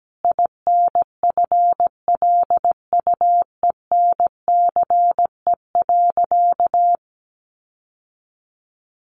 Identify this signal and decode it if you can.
Morse code, heard as a keyed tone, about 17 wpm, 700 Hz. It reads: INFLUENCE.
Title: Sentences from Top 1100 Words 17wpm